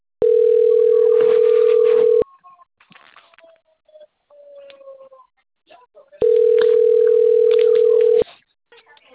Ringback tone (North America), 440 + 480 Hz.
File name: out- - - - - .46099